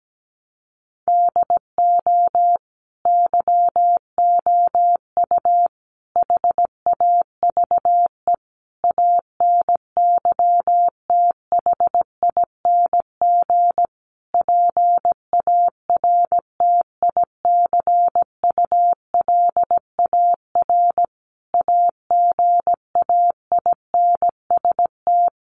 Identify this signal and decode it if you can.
Morse code, heard as a keyed tone, about 17 wpm, 700 Hz. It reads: DO YOU HAVE ANYTHING PARTICULAR AGAINST
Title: Sentences from Top 1000 Words 17wpm